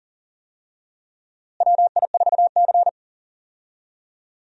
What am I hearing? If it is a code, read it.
Morse code, heard as a keyed tone, about 39 wpm, 700 Hz. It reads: WI4C
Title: USA Call Signs - 4 Characters - Repeated 3x - 40wpm